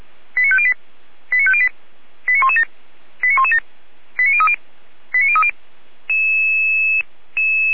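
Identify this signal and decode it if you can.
Five-tone selective call (ZVEI), heard as quick runs of five tones, tones of about 70 ms. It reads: 89508 89508 89117 89117 89030 89030
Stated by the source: ZVEI five-tone selective call received on an SDR.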